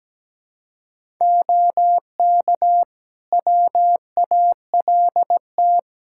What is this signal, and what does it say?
Morse code, heard as a keyed tone, about 17 wpm, 700 Hz. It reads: OK WALT
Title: Practice QSO - Exchange 3 - More Info - 17wpm